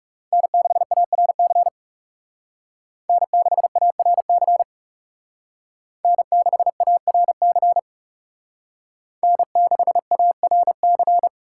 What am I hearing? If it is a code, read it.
Morse code, heard as a keyed tone, about 35 wpm, 700 Hz. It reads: N6ARC N6ARC N6ARC N6ARC
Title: Speed Racing - USA 1x3 Call Signs 30wpm